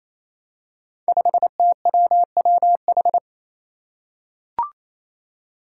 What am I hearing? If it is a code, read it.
Morse code, heard as a keyed tone, about 28 wpm, 700 Hz. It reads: 5TWWH
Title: Five Challenging Letter-Number Combinations 28wpm